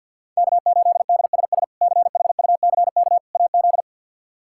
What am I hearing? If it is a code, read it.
Morse code, heard as a keyed tone, about 49 wpm, 690 Hz. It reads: K8DSS CHUCK AZ
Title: K1USN SST QSO Practice - 50wpm